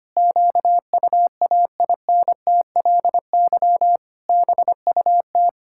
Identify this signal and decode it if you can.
Morse code, heard as a keyed tone, about 25 wpm, 700 Hz. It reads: QUAINTLY BUT
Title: Pangram Challenge v2 - 25wpm